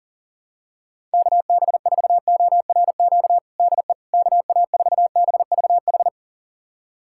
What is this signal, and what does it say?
Morse code, heard as a keyed tone, about 40 wpm, 700 Hz. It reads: KB4ORQ DE KA4BVH
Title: USA Callsign DE Callsign - 6 Characters - 40wpm